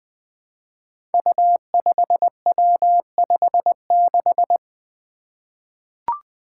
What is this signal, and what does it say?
Morse code, heard as a keyed tone, about 20 wpm, 700 Hz. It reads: U5W56
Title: Five Challenging Letter-Number Combinations 20wpm